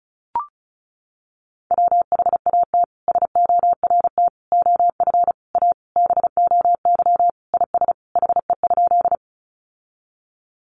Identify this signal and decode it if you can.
Morse code, heard as a keyed tone, about 34 wpm, 700 Hz. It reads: WHAT SORT OF A BOY IS HE?
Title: Sentences from Top 1300 Words 35wpm